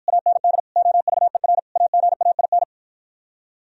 Morse code, heard as a keyed tone, about 53 wpm, 700 Hz. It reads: AND OVER AGAIN